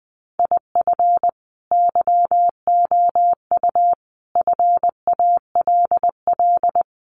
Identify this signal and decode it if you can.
Morse code, heard as a keyed tone, about 20 wpm, 700 Hz. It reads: IF YOU FALL
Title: A Sound Of Thunder 20wpm